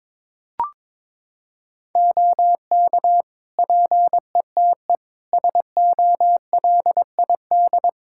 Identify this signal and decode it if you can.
Morse code, heard as a keyed tone, about 22 wpm, 700 Hz. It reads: OK PETE SOLID
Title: Practice QSO - Exchange 3 - More Info - 22wpm